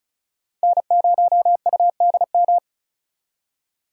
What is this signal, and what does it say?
Morse code, heard as a keyed tone, about 47 wpm, spaced as for 41 wpm, 700 Hz. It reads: N0UDM
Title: Speed Racing - USA 1x3 Call Signs 35wpm